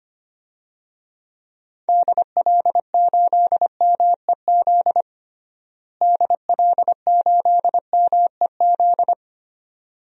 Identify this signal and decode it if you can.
Morse code, heard as a keyed tone, about 25 wpm, 700 Hz. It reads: DL8MEZ DL8MEZ
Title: DXCC-100 Call Signs - 6 Characters - Repeated 2x - 25wpm